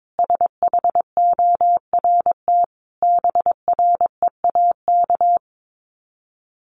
Morse code, heard as a keyed tone, about 22 wpm, 700 Hz. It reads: SHORT BREAK
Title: Sets of 5 Words Encore2 22wpm